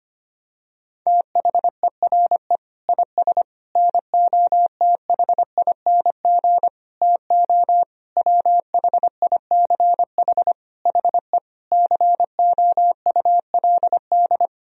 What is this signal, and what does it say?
Morse code, heard as a keyed tone, about 25 wpm, 700 Hz. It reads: THERE IS NOTHING TO WHICH HE COULD
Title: Sentences from Top 1000 Words - Review Part 3 - 25wpm